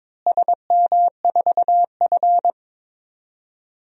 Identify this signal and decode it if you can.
Morse code, heard as a keyed tone, about 22 wpm, 700 Hz. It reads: SM4F
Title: DXCC-100 Call Signs - 4 Characters - Repeated 3x - 22wpm